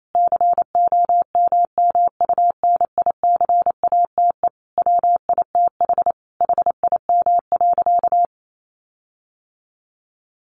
Morse code, heard as a keyed tone, about 28 wpm, 700 Hz. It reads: COMMUNICATE WITH HIM.
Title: Sentences from Top 1300 Words 28wpm